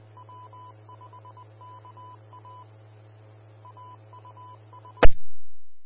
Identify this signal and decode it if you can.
Morse code, heard as a keyed tone, about 20 wpm, 1000 Hz. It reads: W5KA AUS
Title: Austin, TX Amateur Repeaters